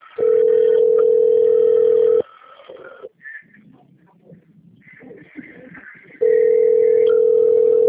Ringback tone (North America), 440 + 480 Hz.